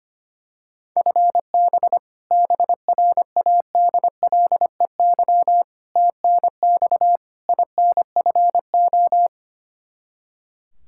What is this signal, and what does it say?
Morse code, heard as a keyed tone, about 25 wpm, 700 Hz. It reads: FB BRADLEY TNX INFO
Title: Practice QSO - Exchange 3 - More Info - 25wpm